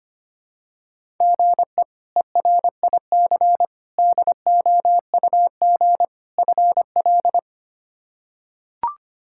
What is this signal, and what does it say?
Morse code, heard as a keyed tone, about 25 wpm, 700 Hz. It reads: GE ERIC DOUG FL
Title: K1USN SST QSO Practice - 25wpm